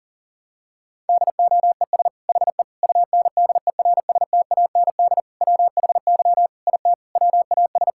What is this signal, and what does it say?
Morse code, heard as a keyed tone, about 40 wpm, 700 Hz. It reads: DOES HE UNDERSTAND WHY IT WAS
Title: Sentences from Top 1000 Words - Review Part 5 - 40wpm